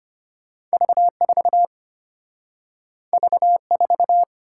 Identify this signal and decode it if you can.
Morse code, heard as a keyed tone, about 34 wpm, 700 Hz. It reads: V4 V4